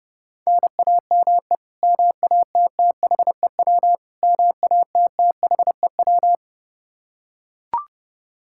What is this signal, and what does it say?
Morse code, heard as a keyed tone, about 30 wpm, 700 Hz. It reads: NAME MATTHEW MATTHEW
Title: K1USN SST QSO Practice 30wpm